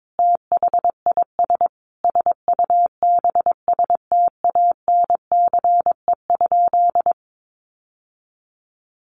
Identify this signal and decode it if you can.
Morse code, heard as a keyed tone, about 22 wpm, 700 Hz. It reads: THIS SUBSTANCE?
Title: Sentences from Top 1500 Words 22wpm